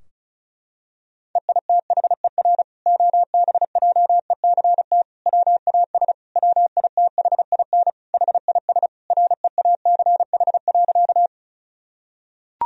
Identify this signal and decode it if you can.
Morse code, heard as a keyed tone, about 34 wpm, 700 Hz. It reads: EITHER OBJECT WAS WITHIN HIS REACH.
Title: Sentences from Top 1000 Words - Review Part 4 - 35wpm